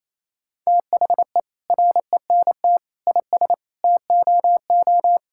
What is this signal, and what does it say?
Morse code, heard as a keyed tone, about 28 wpm, 700 Hz. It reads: THE RENT IS TOO
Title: Sentences from Top 1000 Words - Review Part 5 - 28wpm